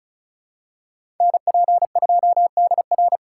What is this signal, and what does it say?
Morse code, heard as a keyed tone, about 36 wpm, 700 Hz. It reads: NP2DR
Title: USA Call Signs - 5 Characters - 4x Character Spacing - 35wpm